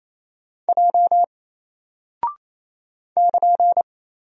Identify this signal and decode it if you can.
Morse code, heard as a keyed tone, about 28 wpm, 700 Hz. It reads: J <KN>
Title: Single Character (Letter:Number:Punctuation:Prosign) Alt 28wpm